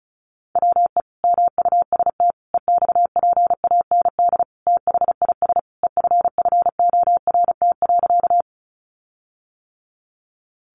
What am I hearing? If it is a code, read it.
Morse code, heard as a keyed tone, about 34 wpm, 700 Hz. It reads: WE MUST EXPAND THIS EFFORT.
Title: Sentences from Top 1400 Words 35wpm